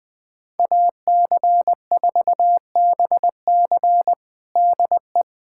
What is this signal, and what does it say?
Morse code, heard as a keyed tone, about 20 wpm, 700 Hz. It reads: AC4BC DE
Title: USA Callsign DE Callsign - 5 Characters - 20wpm